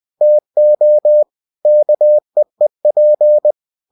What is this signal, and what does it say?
Morse code, heard as a keyed tone, about 20 wpm, 600 Hz. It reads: TO KEEP
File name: cw-books / winnie-the-pooh / winnie-20wpm-page-0051